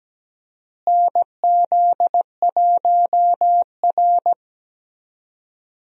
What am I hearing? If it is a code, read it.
Morse code, heard as a keyed tone, about 17 wpm, 700 Hz. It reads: NZ1R
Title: USA Call Signs - 4 Characters - Repeated 3x - 17wpm